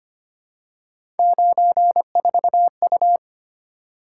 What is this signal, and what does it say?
Morse code, heard as a keyed tone, about 25 wpm, 700 Hz. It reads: 94U
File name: Three Letter-Number Combinations - 25wpm